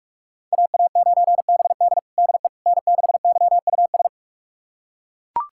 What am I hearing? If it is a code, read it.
Morse code, heard as a keyed tone, about 41 wpm, 700 Hz. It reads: AA9BD DE N6YUS
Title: USA Callsign DE Callsign 45wpm